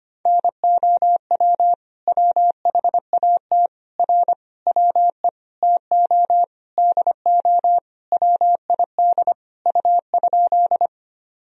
Morse code, heard as a keyed tone, about 25 wpm, 700 Hz. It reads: NOW WHAT R WE TO DO WID U?
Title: Sentences from Top 100 Words and Abbreviations 25wpm